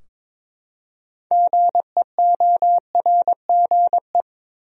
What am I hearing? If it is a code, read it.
Morse code, heard as a keyed tone, about 22 wpm, 700 Hz. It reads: GEORGE
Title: USA Common Names - 22wpm